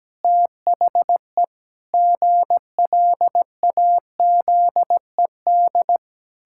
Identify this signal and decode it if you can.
Morse code, heard as a keyed tone, about 17 wpm, 700 Hz. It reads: THE GLAZED
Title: Pangram Challenge v2 - 17wpm